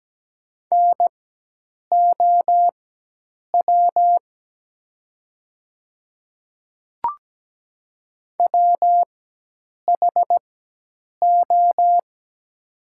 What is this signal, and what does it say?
Morse code, heard as a keyed tone, about 17 wpm, spaced as for 8 wpm, 700 Hz. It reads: NOW WHO
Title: 4x Character Spacing - Top 100 Words - 17wpm